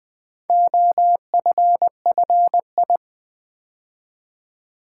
Morse code, heard as a keyed tone, about 20 wpm, 700 Hz. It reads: OFFI
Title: Build A Word - 2 Letters At A Time - 6-Letter Words - 20wpm